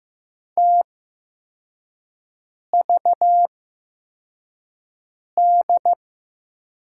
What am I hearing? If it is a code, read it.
Morse code, heard as a keyed tone, about 15 wpm, spaced as for 4 wpm, 700 Hz. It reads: TVD
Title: USA Call Signs - 6 Characters - 8x Character Spacing - 15wpm